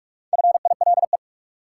Morse code, heard as a keyed tone, about 46 wpm, 700 Hz. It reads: FILE
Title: Sentences from Top 1000 Words - Review Part 4 - New Word Drill - 45wpm